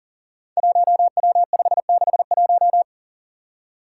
Morse code, heard as a keyed tone, about 38 wpm, 690 Hz. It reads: 1W561
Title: Five Challenging Letter-Number Combinations 40wpm